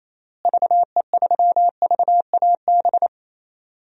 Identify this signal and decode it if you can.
Morse code, heard as a keyed tone, about 28 wpm, 700 Hz. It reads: VE3VAB
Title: DXCC-100 Call Signs - 6 Characters - Repeated 3x - 28wpm